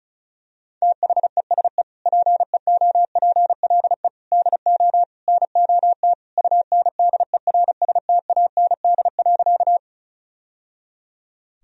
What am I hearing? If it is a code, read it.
Morse code, heard as a keyed tone, about 36 wpm, 700 Hz. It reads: THESE PEOPLE DO NOT UNDERSTAND.
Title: Sentences from Top 1000 Words - Review Part 1 - 35wpm